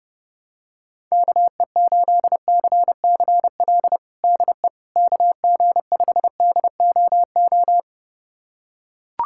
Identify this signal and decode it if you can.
Morse code, heard as a keyed tone, about 30 wpm, 700 Hz. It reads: KE8CCL DE KG5DOO
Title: USA Callsign DE Callsign - 6 Characters - 30wpm